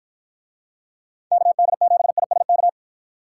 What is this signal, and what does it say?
Morse code, heard as a keyed tone, about 53 wpm, 700 Hz. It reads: KD7ISK